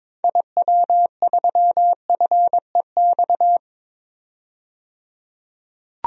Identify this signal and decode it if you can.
Morse code, heard as a keyed tone, about 22 wpm, 700 Hz. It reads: IW3FEX
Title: DXCC-100 Call Signs - 6 Characters - 22wpm